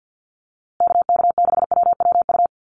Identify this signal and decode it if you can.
Morse code, heard as a keyed tone, about 50 wpm, 700 Hz. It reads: KK6WWU
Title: USA Call Signs Repeated 3x 50wpm